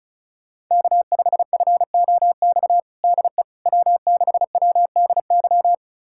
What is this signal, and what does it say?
Morse code, heard as a keyed tone, about 34 wpm, 700 Hz. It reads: K5FOX DE W6WDY
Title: USA Callsign DE Callsign - 5 Characters - 35wpm